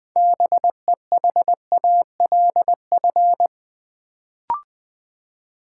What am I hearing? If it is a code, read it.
Morse code, heard as a keyed tone, about 20 wpm, 700 Hz. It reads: BEHALF
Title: Build A Word - 3 Letters At A Time - 6-Letter Words - 20wpm